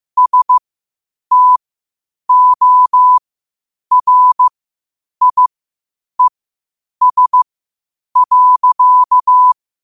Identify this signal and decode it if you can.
Morse code, heard as a keyed tone, about 15 wpm, spaced as for 8 wpm, 1000 Hz. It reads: STORIES.
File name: qotdfiles / 8wpm